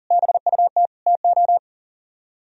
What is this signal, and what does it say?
Morse code, heard as a keyed tone, about 50 wpm, 700 Hz. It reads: BUT TO
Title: Speed-Racing - Sets of 5 Words 40wpm